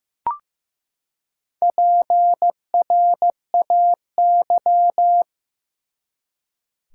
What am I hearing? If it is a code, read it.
Morse code, heard as a keyed tone, about 15 wpm, 700 Hz. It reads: PRAY
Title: Four Letter Words - 15wpm